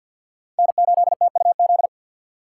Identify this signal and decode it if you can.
Morse code, heard as a keyed tone, about 53 wpm, 700 Hz. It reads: N8TUZ
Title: USA Call Signs - 5 Characters - 3x Character Spacing - 50wpm